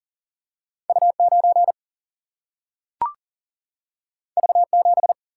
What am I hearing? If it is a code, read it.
Morse code, heard as a keyed tone, about 40 wpm, 700 Hz. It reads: U9 V7
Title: Single-Letter Single-Number 40wpm